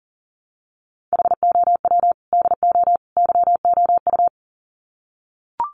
Morse code, heard as a keyed tone, about 40 wpm, 700 Hz. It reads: HOW DO YOU
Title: Sets of 3 Words 40wpm